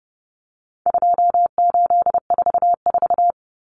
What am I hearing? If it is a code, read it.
Morse code, heard as a keyed tone, about 30 wpm, 700 Hz. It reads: 2844
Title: Numbers - 4-Digit 30wpm